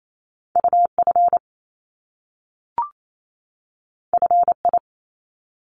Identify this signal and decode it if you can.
Morse code, heard as a keyed tone, about 28 wpm, 700 Hz. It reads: UF FI
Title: Top Two Letter Combinations in Words 28wpm